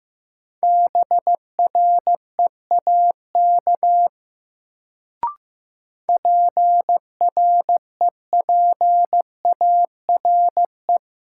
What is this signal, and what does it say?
Morse code, heard as a keyed tone, about 15 wpm, 700 Hz. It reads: BREAK PREPARE
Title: Sentences from Top 900 Words - New Word Drill - 15wpm